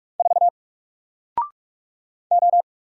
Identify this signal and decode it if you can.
Morse code, heard as a keyed tone, about 44 wpm, 700 Hz. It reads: 4 O
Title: Single Letter-Number Alt 45wpm